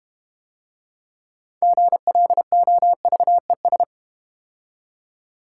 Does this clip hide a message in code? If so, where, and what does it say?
Yes: Morse code from 1.6 to 3.8 s, heard as a keyed tone, about 32 wpm, 700 Hz. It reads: GLOVES